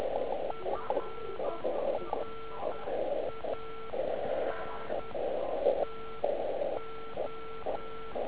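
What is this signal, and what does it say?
Morse code, heard as a keyed tone, about 10 wpm, spaced as for 8 wpm, 1340 Hz. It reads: FRANTO
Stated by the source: Slow (QRS) CW test by beginner OK1IFW during beginner Morse skeds on 3559.5 kHz (80 m).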